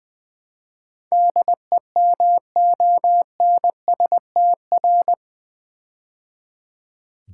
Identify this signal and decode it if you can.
Morse code, heard as a keyed tone, about 20 wpm, 700 Hz. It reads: DEMONSTR
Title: Build A Word - 4 Letters At A Time - 11-Letter Words - 20wpm